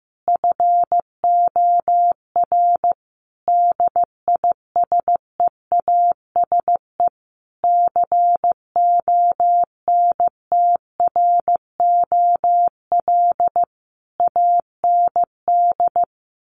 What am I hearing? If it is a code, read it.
Morse code, heard as a keyed tone, about 15 wpm, 700 Hz. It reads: FOR DISEASE CONTROL AND
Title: Sets of 5 Words Encore2 15wpm